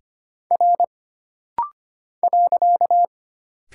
Morse code, heard as a keyed tone, about 25 wpm, 700 Hz. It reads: R .
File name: Single Character (Letter:Number:Punctuation:Prosign) - Rapid-Fire - 25wpm